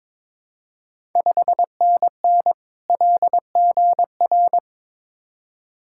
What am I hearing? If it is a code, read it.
Morse code, heard as a keyed tone, about 21 wpm, 700 Hz. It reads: 5NN LGR
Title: Ontario QSO Party Exchange 22wpm